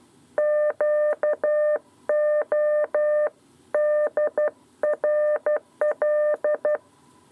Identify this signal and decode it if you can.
Morse code, heard as a keyed tone, about 12 wpm, 600 Hz. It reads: QODRL
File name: 7b lezione cw 2